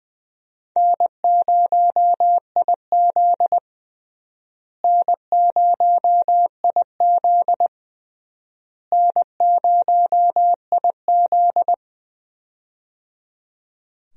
Morse code, heard as a keyed tone, about 21 wpm, 700 Hz. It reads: N0IZ N0IZ N0IZ
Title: USA Call Signs - 4 Characters - Repeated 3x - 20wpm